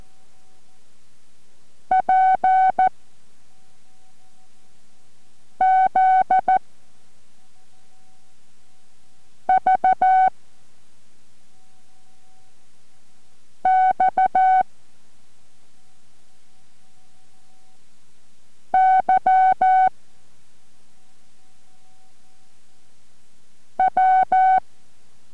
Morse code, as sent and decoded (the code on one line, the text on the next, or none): .--. / --.. / ...- / -..- / -.-- / .--
P Z V X Y W